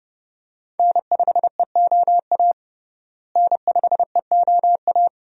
Morse code, heard as a keyed tone, about 30 wpm, 700 Hz. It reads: N5EOA N5EOA
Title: USA Call Signs - 5 Characters - Repeated 2x - 30wpm